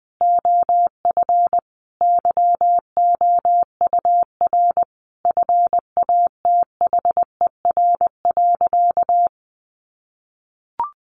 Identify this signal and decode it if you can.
Morse code, heard as a keyed tone, about 20 wpm, 700 Hz. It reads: OF YOUR FATHER.
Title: Sentences from Top 1300 Words 20wpm